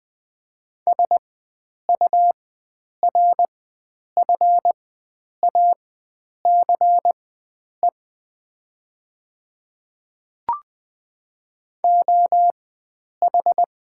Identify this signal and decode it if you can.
Morse code, heard as a keyed tone, about 20 wpm, spaced as for 9 wpm, 700 Hz. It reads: SURFACE OH
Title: 4x Character Spacing - Top 500 Words - 20wpm